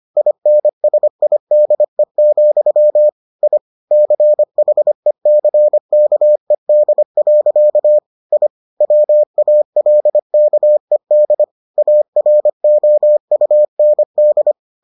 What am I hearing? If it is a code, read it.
Morse code, heard as a keyed tone, about 25 wpm, 600 Hz. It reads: INSIDE, I CHECKED. I WALKED AROUND